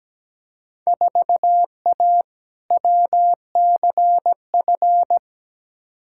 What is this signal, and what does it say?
Morse code, heard as a keyed tone, about 17 wpm, 700 Hz. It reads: 4A WCF
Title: Field Day Exchanges - 17wpm